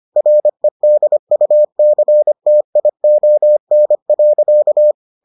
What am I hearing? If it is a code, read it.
Morse code, heard as a keyed tone, about 25 wpm, 600 Hz. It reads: REDUCTION.